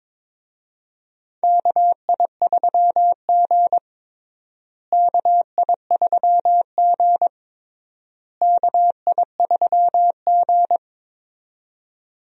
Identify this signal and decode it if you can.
Morse code, heard as a keyed tone, about 22 wpm, 700 Hz. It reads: KI3G KI3G KI3G
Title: USA Call Signs - 4 Characters - Repeated 3x - 22wpm